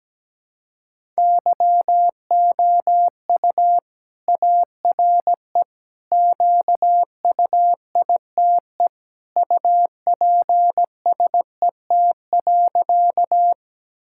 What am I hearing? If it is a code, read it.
Morse code, heard as a keyed tone, about 17 wpm, 700 Hz. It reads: YOU ARE QUITE UPSET.
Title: Sentences from Top 1000 Words - Review Part 5 - 17wpm